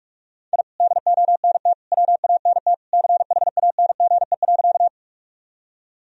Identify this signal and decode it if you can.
Morse code, heard as a keyed tone, about 42 wpm, 700 Hz. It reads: I DONT WANT CHANGE.
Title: Sentences from Top 1000 Words - Review Part 1 - 45wpm